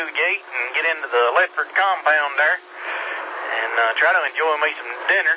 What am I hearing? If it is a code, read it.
Emergency Alert System (SAME) tones, heard as harsh buzzing bursts, with speech. It is heard from 0.5 to 4.3 s.